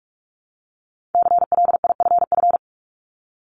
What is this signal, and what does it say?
Morse code, heard as a keyed tone, about 46 wpm, 700 Hz. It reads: CLIFF
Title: Multi-letter Phonemes Repeated 3x 45wpm